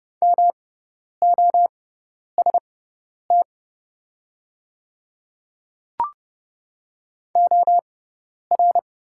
Morse code, heard as a keyed tone, about 30 wpm, spaced as for 10 wpm, 700 Hz. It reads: MOST OR
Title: 6x Character Spacing - Top 200 Words - 30wpm